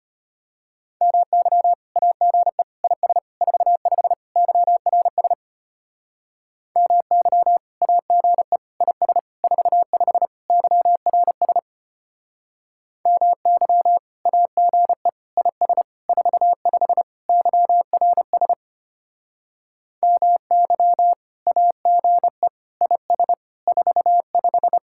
Morse code, heard as a keyed tone, about 32 wpm, 700 Hz. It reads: MY AGE IS 45 YRS MY AGE IS 45 YRS MY AGE IS 45 YRS MY AGE IS 45